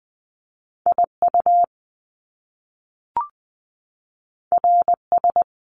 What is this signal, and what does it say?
Morse code, heard as a keyed tone, about 20 wpm, 700 Hz. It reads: I U R S
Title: Top Two Letter Combinations in Words 20wpm